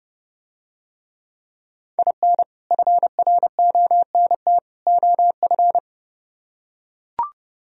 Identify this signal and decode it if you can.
Morse code, heard as a keyed tone, about 31 wpm, 700 Hz. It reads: IN FRONT OF